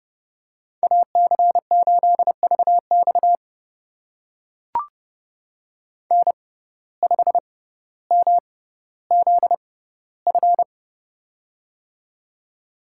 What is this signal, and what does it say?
Morse code, heard as a keyed tone, about 30 wpm, spaced as for 15 wpm, 700 Hz. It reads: AC8VX N5MZF
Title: USA Call Signs - 5 Characters - 6x Character Spacing - 30wpm